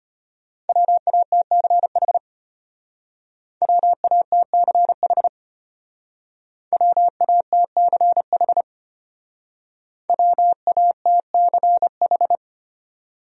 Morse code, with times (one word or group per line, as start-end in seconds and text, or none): 0.7-2.2 WATCH
3.6-5.3 WATCH
6.7-8.6 WATCH
10.1-12.4 WATCH